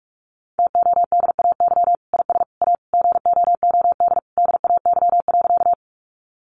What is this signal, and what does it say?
Morse code, heard as a keyed tone, about 44 wpm, 700 Hz. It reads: TODAY IS A GOOD DAY.
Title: Sentences from Top 600 Words 45wpm